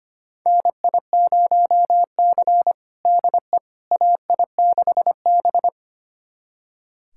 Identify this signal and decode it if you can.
Morse code, heard as a keyed tone, about 25 wpm, 700 Hz. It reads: NI0C DE AI6B
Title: USA Callsign DE Callsign - 4 Characters - 25wpm